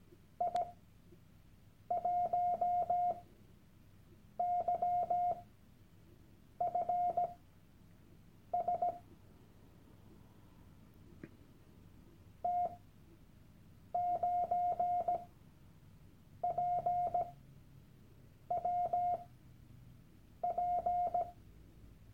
Morse code, heard as a keyed tone, about 17 wpm, spaced as for 6 wpm, 710 Hz. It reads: I1YFS T9PWP
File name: Workshop Dateien / CW-Online-Kurs / Lektion 001